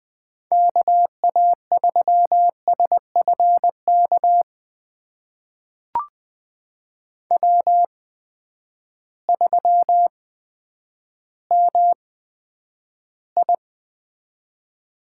Morse code, heard as a keyed tone, about 20 wpm, spaced as for 10 wpm, 700 Hz. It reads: KA3SFK W3MI
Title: USA Call Signs - 8x Character Spacing - 20wpm